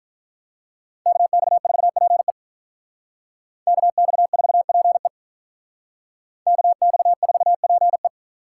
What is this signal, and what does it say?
Morse code, heard as a keyed tone, about 45 wpm, 700 Hz. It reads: KX4PE KX4PE KX4PE